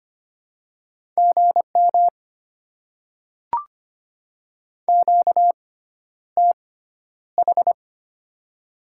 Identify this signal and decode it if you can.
Morse code, heard as a keyed tone, about 25 wpm, spaced as for 11 wpm, 700 Hz. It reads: GM QTH
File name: Top 20 QSO Elements - 6x Character Spacing - 25wpm